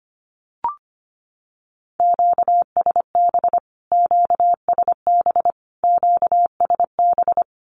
Morse code, heard as a keyed tone, about 25 wpm, 700 Hz. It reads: QSB QSB QSB
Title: Q-Codes Repeated 3x 25wpm